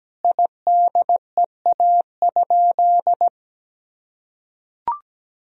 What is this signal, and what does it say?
Morse code, heard as a keyed tone, about 17 wpm, 700 Hz. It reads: IDEA?
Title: Sentences from Top 400 Words and Abbreviations 17wpm